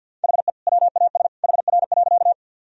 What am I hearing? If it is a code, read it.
Morse code, heard as a keyed tone, about 50 wpm, 700 Hz. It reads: HE WAS HR.
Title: Sentences from Top 200 Words and Abbreviations 50wpm